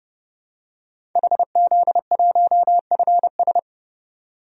Morse code, heard as a keyed tone, about 32 wpm, 700 Hz. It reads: HZ1FS
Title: DXCC-100 Call Signs - 5 Characters - 30wpm